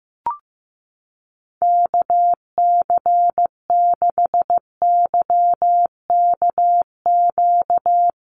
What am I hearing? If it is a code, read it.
Morse code, heard as a keyed tone, about 15 wpm, 700 Hz. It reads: KC6YKQ
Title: USA Call Signs 15wpm